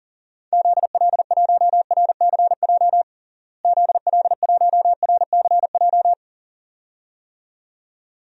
Morse code, heard as a keyed tone, about 40 wpm, 700 Hz. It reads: ZL1RCJ ZL1RCJ
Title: DXCC-100 Call Signs - 6 Characters - Repeated 2x - 40wpm